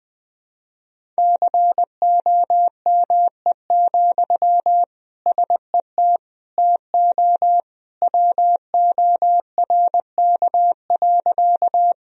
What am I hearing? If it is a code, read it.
Morse code, heard as a keyed tone, about 20 wpm, 700 Hz. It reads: COME, SET TO WORK.
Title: Sentences from Top 1000 Words - Review Part 1 - 20wpm